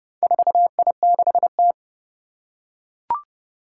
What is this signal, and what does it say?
Morse code, heard as a keyed tone, about 30 wpm, 700 Hz. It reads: 4I6T
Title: Four Challenging Letter-Number Combinations 30wpm